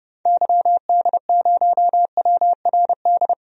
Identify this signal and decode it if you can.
Morse code, heard as a keyed tone, about 30 wpm, 700 Hz. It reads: YD0WRD
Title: DXCC-100 Call Signs - 6 Characters - 30wpm